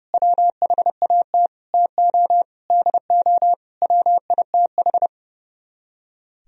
Morse code, heard as a keyed tone, about 33 wpm, 700 Hz. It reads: WHAT TO DO WITH
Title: Speed-Racing - Sets of 5 Words 30wpm